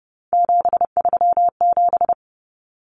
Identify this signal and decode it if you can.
Morse code, heard as a keyed tone, about 30 wpm, 700 Hz. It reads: 737
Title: Common Cut Numbers - 3-Digit 30wpm